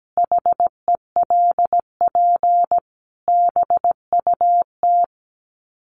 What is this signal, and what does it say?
Morse code, heard as a keyed tone, about 17 wpm, 700 Hz. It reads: HELP BUT
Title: Sets of 5 Words Encore2 17wpm